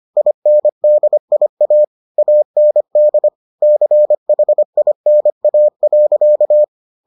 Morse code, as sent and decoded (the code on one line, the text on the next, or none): .. -. -.. .. .- / .- -. -.. / -.-. .... .. -. .- .-.-.-
INDIA AND CHINA.